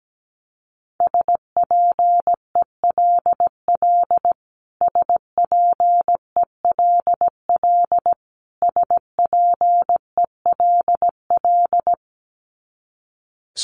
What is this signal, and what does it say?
Morse code, heard as a keyed tone, about 17 wpm, 700 Hz. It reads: SPELL SPELL SPELL
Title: Top 200 Words Repeated 17wpm